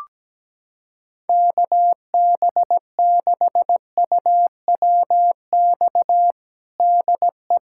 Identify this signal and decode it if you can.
Morse code, heard as a keyed tone, about 17 wpm, 700 Hz. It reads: KB6UWX DE
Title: USA Callsign DE Callsign 17wpm